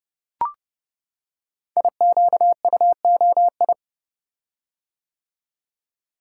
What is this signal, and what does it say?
Morse code, heard as a keyed tone, about 30 wpm, 700 Hz. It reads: IQUOI
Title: Code Groups - 5 Letters - 30wpm